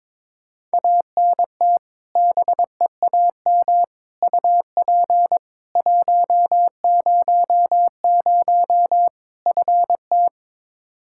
Morse code, heard as a keyed tone, about 22 wpm, 700 Hz. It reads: ANT BEAM UP 100 FT
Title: K1USN SST QSO Practice 22wpm